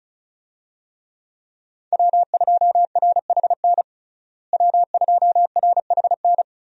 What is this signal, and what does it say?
Morse code, heard as a keyed tone, about 36 wpm, 700 Hz. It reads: W2RHN W2RHN
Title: USA Call Signs - 5 Characters - Repeated 2x - 35wpm